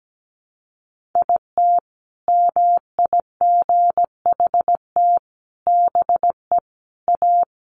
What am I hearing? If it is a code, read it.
Morse code, heard as a keyed tone, about 17 wpm, 700 Hz. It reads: IT MIGHT BE A
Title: Sentences from Top 1500 Words 17wpm